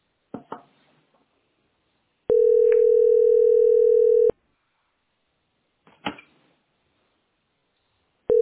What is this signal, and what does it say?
Ringback tone (North America), 440 + 480 Hz.